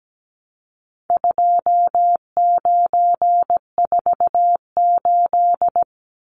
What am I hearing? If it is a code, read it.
Morse code, heard as a keyed tone, about 17 wpm, 700 Hz. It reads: 2948
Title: Numbers - 4-Digit 17wpm